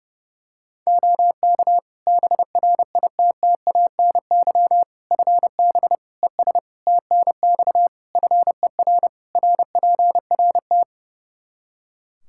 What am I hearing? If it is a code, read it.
Morse code, heard as a keyed tone, about 30 wpm, 700 Hz. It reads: OK BRITTANY FB ES TNX FER RPRT
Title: Practice QSO - Exchange 2 - Info - 30wpm